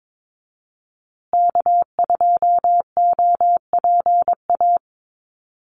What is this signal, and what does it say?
Morse code, heard as a keyed tone, about 22 wpm, 700 Hz. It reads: K2OPA
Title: USA Call Signs Repeated 3x 22wpm